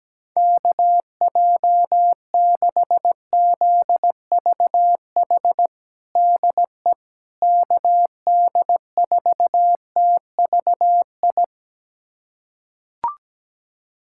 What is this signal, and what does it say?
Morse code, heard as a keyed tone, about 17 wpm, 700 Hz. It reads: KJ6ZVH DE KD4TVI
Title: USA Callsign DE Callsign 17wpm